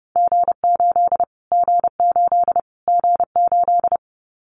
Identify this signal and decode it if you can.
Morse code, heard as a keyed tone, about 30 wpm, 700 Hz. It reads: G8 G8 G8
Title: Callsign Prefixes for Top 25 DXCC - Repeated 3x - 30wpm